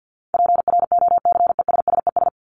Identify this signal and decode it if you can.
Morse code, heard as a keyed tone, about 50 wpm, 700 Hz. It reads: PROCESSES
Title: ES:S Suffix - Top 100 Words with Base Words 50wpm